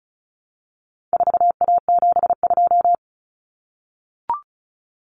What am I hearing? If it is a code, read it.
Morse code, heard as a keyed tone, about 34 wpm, 700 Hz. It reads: 4A72
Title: Common Cut Numbers - 4-Digit 35wpm